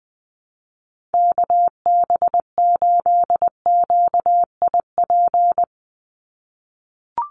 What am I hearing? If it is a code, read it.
Morse code, heard as a keyed tone, about 20 wpm, 700 Hz. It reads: KB8QIP
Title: USA Call Signs 20wpm